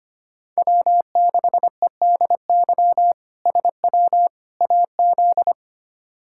Morse code, heard as a keyed tone, about 25 wpm, 700 Hz. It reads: W6EDY SW AZ
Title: K1USN SST QSO Practice - 25wpm